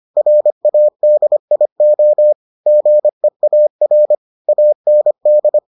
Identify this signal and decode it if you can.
Morse code, heard as a keyed tone, about 25 wpm, 600 Hz. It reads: RADIO GEAR AND